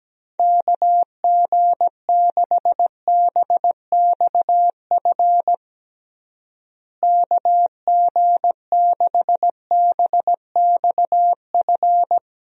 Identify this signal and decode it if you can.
Morse code, heard as a keyed tone, about 17 wpm, 700 Hz. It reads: KG6BXF KG6BXF
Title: USA Call Signs - 6 Characters - Repeated 3x - 17wpm